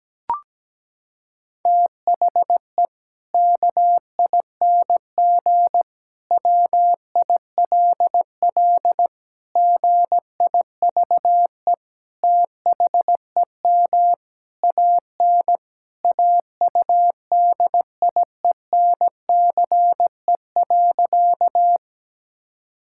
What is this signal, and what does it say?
Morse code, heard as a keyed tone, about 17 wpm, 700 Hz. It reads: THE KING WILL GIVE THEM AN AUDIENCE.
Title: Sentences from Top 1000 Words - Review Part 5 - 17wpm